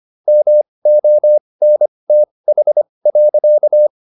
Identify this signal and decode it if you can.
Morse code, heard as a keyed tone, about 25 wpm, spaced as for 20 wpm, 600 Hz. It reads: MONTH.